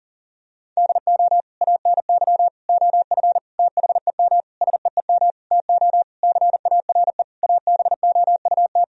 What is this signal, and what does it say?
Morse code, heard as a keyed tone, about 40 wpm, 700 Hz. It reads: DO ANY OF THEM SEEM TO CARE ABOUT
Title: Sentences from Top 1000 Words - Review Part 3 - 40wpm